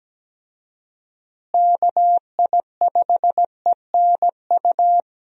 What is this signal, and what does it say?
Morse code, heard as a keyed tone, about 17 wpm, 700 Hz. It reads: KI5ENU
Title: USA Call Signs - 6 Characters - Repeated 2x - 17wpm